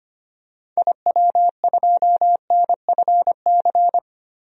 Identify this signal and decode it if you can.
Morse code, heard as a keyed tone, about 25 wpm, 700 Hz. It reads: IW2NFC
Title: DXCC-100 Call Signs - 6 Characters - 25wpm